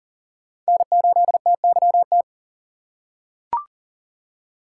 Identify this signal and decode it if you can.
Morse code, heard as a keyed tone, about 40 wpm, 700 Hz. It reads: N8TYT
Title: USA Call Signs - 5 Characters - 3x Character Spacing - 40wpm